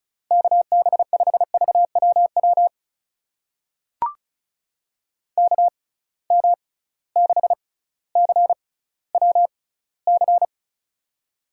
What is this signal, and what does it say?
Morse code, heard as a keyed tone, about 36 wpm, spaced as for 18 wpm, 700 Hz. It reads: KB5VWW KM6CWC
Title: USA Call Signs - 6 Characters - 6x Character Spacing - 35wpm